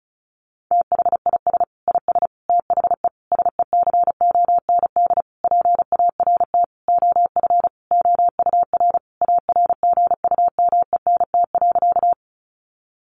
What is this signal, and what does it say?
Morse code, heard as a keyed tone, about 34 wpm, 700 Hz. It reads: THIS IS THE SECOND PART OF OUR ARGUMENT.
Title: Sentences from Top 1400 Words 35wpm